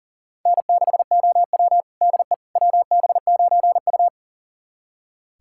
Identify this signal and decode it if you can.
Morse code, heard as a keyed tone, about 37 wpm, 700 Hz. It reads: N6OW DE WB9U
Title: USA Callsign DE Callsign - 4 Characters - 40wpm